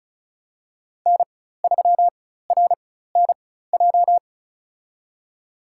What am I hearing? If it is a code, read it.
Morse code, heard as a keyed tone, about 34 wpm, spaced as for 16 wpm, 700 Hz. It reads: N3RNJ
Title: USA Call Signs - 5 Characters - 4x Character Spacing - 35wpm